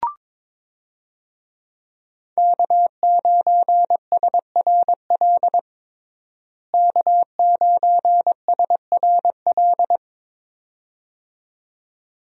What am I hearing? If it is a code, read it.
Morse code, heard as a keyed tone, about 21 wpm, 700 Hz. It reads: K9SRL K9SRL
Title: USA Call Signs - 5 Characters - Repeated 2x - 22wpm